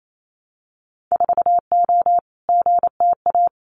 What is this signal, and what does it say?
Morse code, heard as a keyed tone, about 28 wpm, 700 Hz. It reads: 4O GTA
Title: Winter Field Day Exchanges 28wpm